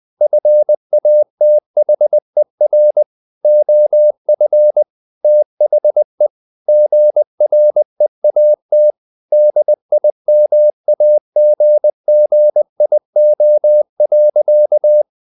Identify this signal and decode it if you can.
Morse code, heard as a keyed tone, about 20 wpm, 600 Hz. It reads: FATHER OF THE GREAT DIMAGGIO.
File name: the old man and the sea part 2 20wpm